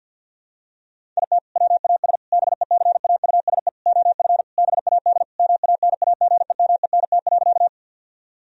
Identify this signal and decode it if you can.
Morse code, heard as a keyed tone, about 50 wpm, 700 Hz. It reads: IT WAS BECAUSE OF BAD MANAGEMENT.
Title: Sentences from Top 1000 Words - Review Part 4 - 50wpm